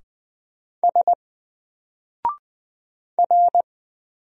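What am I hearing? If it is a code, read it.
Morse code, heard as a keyed tone, about 20 wpm, 700 Hz. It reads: S R
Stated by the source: Single letter and number Morse drill at 20 wpm, from N4LCD.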